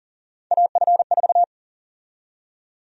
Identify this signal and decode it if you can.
Morse code, heard as a keyed tone, about 40 wpm, 700 Hz. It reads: AF4
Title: USA Call Sign Prefixes - 40wpm